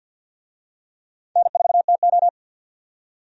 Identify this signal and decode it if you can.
Morse code, heard as a keyed tone, about 52 wpm, 700 Hz. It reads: N4TO
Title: USA Call Signs - 4 Characters - 50wpm